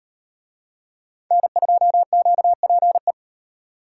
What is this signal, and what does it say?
Morse code, heard as a keyed tone, about 38 wpm, 700 Hz. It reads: N2QPE